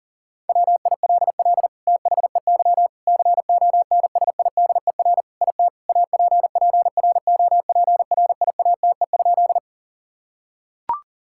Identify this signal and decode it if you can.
Morse code, heard as a keyed tone, about 40 wpm, 700 Hz. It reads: WILL THEY CONSIDER IT APPROPRIATE?
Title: Sentences from Top 1000 Words - Review Part 5 - 40wpm